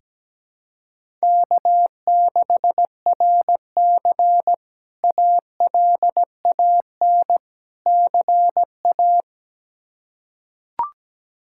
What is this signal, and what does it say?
Morse code, heard as a keyed tone, about 17 wpm, 700 Hz. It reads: K6RC ALAN CA
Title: K1USN SST QSO Practice - 17wpm